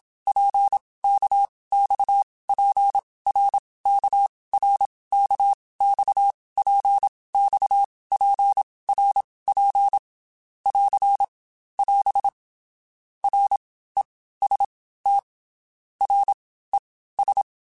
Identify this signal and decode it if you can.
Morse code, heard as a keyed tone, about 27 wpm, spaced as for 18 wpm, 800 Hz. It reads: PKXPRKRKXPXPRP <AR> <AS> REST RES
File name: Basic Letter Groups By7Groups / RKPX / RKPX@25